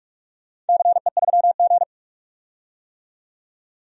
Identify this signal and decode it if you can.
Morse code, heard as a keyed tone, about 47 wpm, 700 Hz. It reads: CE3G
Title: DXCC-100 Call Signs - 4 Characters - 45wpm